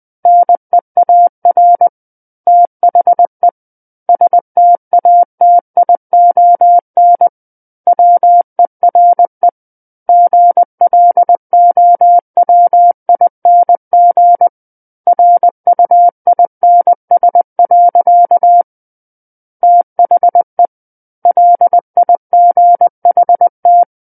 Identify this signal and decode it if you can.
Morse code, heard as a keyed tone, about 20 wpm, 700 Hz. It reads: NEAR THE STATION WERE GLOWING RUINS. THE LIGHT
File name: 20 WPM